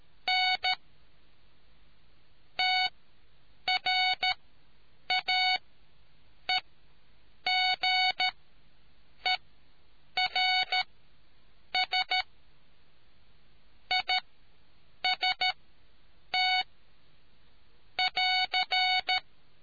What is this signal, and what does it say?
Morse code, heard as a keyed tone, about 13 wpm, spaced as for 7 wpm, 740 Hz. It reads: N TRAEGERS IST <AR>